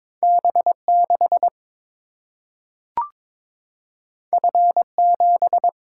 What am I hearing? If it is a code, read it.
Morse code, heard as a keyed tone, about 22 wpm, 700 Hz. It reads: B6 F7
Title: Single-Letter Single-Number 22wpm